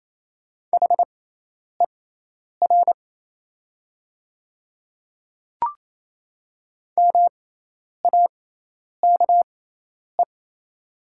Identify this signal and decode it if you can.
Morse code, heard as a keyed tone, about 27 wpm, spaced as for 10 wpm, 690 Hz. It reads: HER MAKE
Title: 6x Character Spacing - Top 100 Words - 28wpm